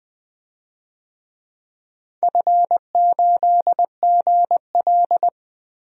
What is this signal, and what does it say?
Morse code, heard as a keyed tone, about 20 wpm, 700 Hz. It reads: F8GL
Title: DXCC-100 Call Signs - 4 Characters - 20wpm